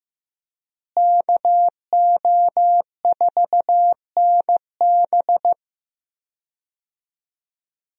Morse code, heard as a keyed tone, about 15 wpm, 700 Hz. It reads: KO4NB
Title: USA Call Signs - 5 Characters - Repeated 3x - 15wpm